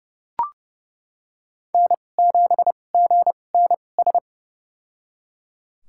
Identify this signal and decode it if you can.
Morse code, heard as a keyed tone, about 30 wpm, spaced as for 22 wpm, 700 Hz. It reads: N7GNS
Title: USA Call Signs - 5 Characters - 2x Character Spacing - 30wpm